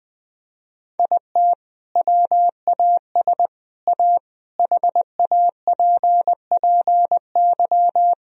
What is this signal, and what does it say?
Morse code, heard as a keyed tone, about 21 wpm, 700 Hz. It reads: IT WAS A HAPPY
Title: Sentences from Top 1000 Words - Review Part 1 - 20wpm